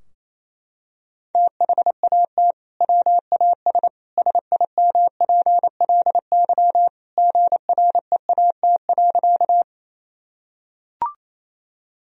Morse code, heard as a keyed tone, about 28 wpm, 700 Hz. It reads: THAT WAS SIMPLY GREAT.
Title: Sentences from Top 1000 Words - Review Part 4 - 28wpm